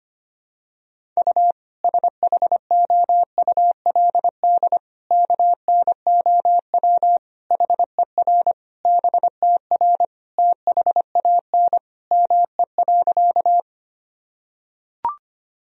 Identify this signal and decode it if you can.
Morse code, heard as a keyed tone, about 25 wpm, 700 Hz. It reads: U SHOULD KNOW HER BTR THAN ME.
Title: Sentences from Top 400 Words and Abbreviations 25wpm